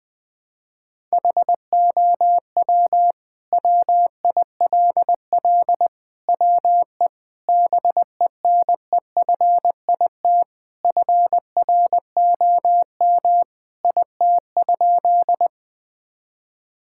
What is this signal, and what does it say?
Morse code, heard as a keyed tone, about 20 wpm, 700 Hz. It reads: HOW WILL WE BENEFIT FROM IT?
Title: Sentences from Top 1000 Words - Review Part 6 - 20wpm